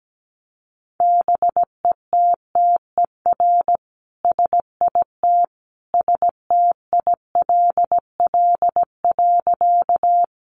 Morse code, heard as a keyed tone, about 17 wpm, 700 Hz. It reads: BETTER SIT STILL.
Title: Sentences from Top 500 Words 17wpm